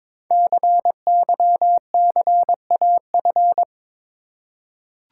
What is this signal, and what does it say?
Morse code, heard as a keyed tone, about 22 wpm, 700 Hz. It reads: CYCAF
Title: Code Groups - 5 Letters - 22wpm